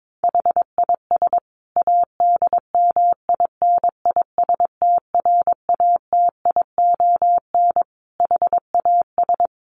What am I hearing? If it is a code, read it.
Morse code, heard as a keyed tone, about 22 wpm, 700 Hz. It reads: HIS ADMINISTRATION HAS
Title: Sentences from Top 1300 Words 22wpm